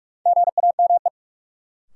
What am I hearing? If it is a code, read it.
Morse code, heard as a keyed tone, about 43 wpm, 700 Hz. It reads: GAME
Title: Sentences from Top 500 Words - New Word Drill - 45wpm